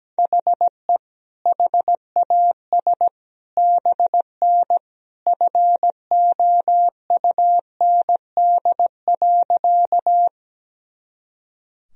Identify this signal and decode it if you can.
Morse code, heard as a keyed tone, about 17 wpm, 700 Hz. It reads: HE HAS BN FOUND.
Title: Sentences from Top 300 Words and Abbreviations 17wpm